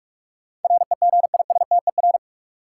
Morse code, heard as a keyed tone, about 46 wpm, 700 Hz. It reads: REGISTER
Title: Sentences from Top 1200 Words - New Word Drill - 45wpm